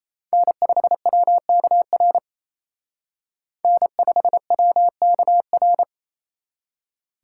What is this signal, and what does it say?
Morse code, heard as a keyed tone, about 35 wpm, 700 Hz. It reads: N5WKR N5WKR